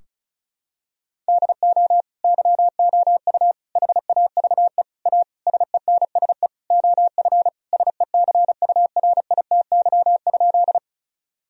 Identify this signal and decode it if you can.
Morse code, heard as a keyed tone, about 36 wpm, 700 Hz. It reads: DO YOU HAVE A SENSE OF SECURITY?
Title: Sentences from Top 1000 Words - Review Part 4 - 35wpm